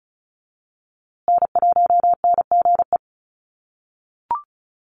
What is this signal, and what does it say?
Morse code, heard as a keyed tone, about 34 wpm, 700 Hz. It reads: N1NGE
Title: USA Call Signs Repeated 3x 35wpm